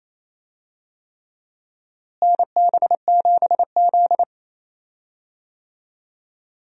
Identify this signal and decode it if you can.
Morse code, heard as a keyed tone, about 27 wpm, 700 Hz. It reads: NB7Z
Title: USA Call Signs - 4 Characters - Repeated 3x - 28wpm